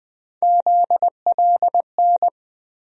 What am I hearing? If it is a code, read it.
Morse code, heard as a keyed tone, about 20 wpm, 700 Hz. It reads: ZLN
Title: USA Call Sign Postfixes - 3 Letters 20wpm